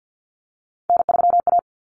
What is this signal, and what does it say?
Morse code, heard as a keyed tone, about 50 wpm, 700 Hz. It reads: N3A
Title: Common Cut Numbers - 3-Digit 50wpm